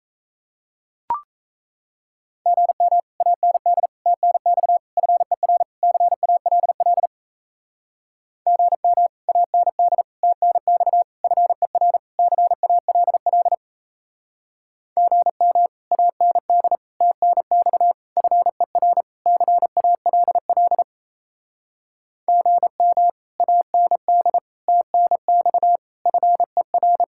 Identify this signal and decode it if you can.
Morse code, heard as a keyed tone, about 37 wpm, 700 Hz. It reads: GM AND TNX FER CALL GM AND TNX FER CALL GM AND TNX FER CALL GM AND TNX FER